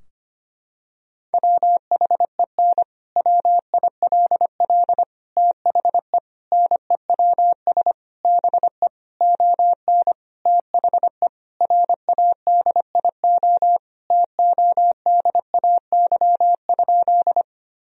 Morse code, heard as a keyed tone, about 25 wpm, 700 Hz. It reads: WHEN WILL THE NEWS BE ON THE RADIO TODAY?
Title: Sentences from Top 1000 Words - Review Part 3 - 25wpm